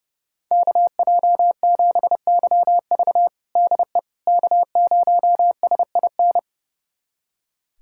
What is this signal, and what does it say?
Morse code, heard as a keyed tone, about 30 wpm, 700 Hz. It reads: KJ7YV DE K0SIN
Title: USA Callsign DE Callsign - 5 Characters - 30wpm